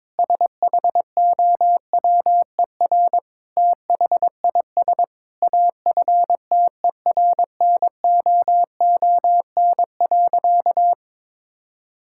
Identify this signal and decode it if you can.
Morse code, heard as a keyed tone, about 22 wpm, 700 Hz. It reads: SHOWER THIS AFTERNOON.
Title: Sentences from Top 1600 Words 22wpm